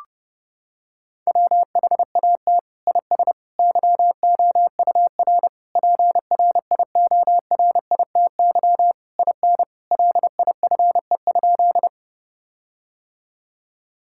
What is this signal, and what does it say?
Morse code, heard as a keyed tone, about 30 wpm, 700 Hz. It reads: WHAT IS YOUR PRIORITY IN LIFE?
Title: Sentences from Top 1600 Words 30wpm